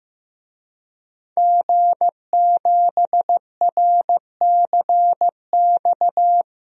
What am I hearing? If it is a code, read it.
Morse code, heard as a keyed tone, about 15 wpm, 700 Hz. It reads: G7RCX
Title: DXCC-100 Call Signs - 5 Characters - Repeated 3x - 15wpm